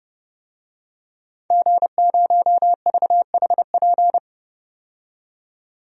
Morse code, heard as a keyed tone, about 30 wpm, 700 Hz. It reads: G0VHP
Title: DXCC-100 Call Signs - 5 Characters - 30wpm